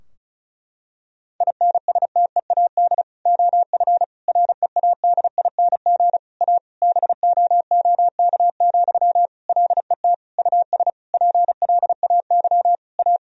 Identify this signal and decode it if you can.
Morse code, heard as a keyed tone, about 35 wpm, 700 Hz. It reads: INSTEAD OF READING A BOOK, LET US PLAY A
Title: Sentences from Top 1000 Words - Review Part 4 - 35wpm